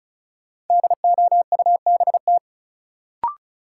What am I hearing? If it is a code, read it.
Morse code, heard as a keyed tone, about 36 wpm, 700 Hz. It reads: DOUBT
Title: Sentences from Top 1500 Words - New Word Drill - 35wpm